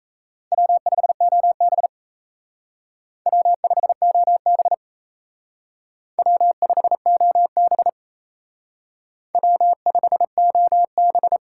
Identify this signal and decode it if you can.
Morse code, heard as a keyed tone, about 35 wpm, 700 Hz. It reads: W5OB W5OB W5OB W5OB
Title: Speed Racing - USA 1x2 Call Signs 28wpm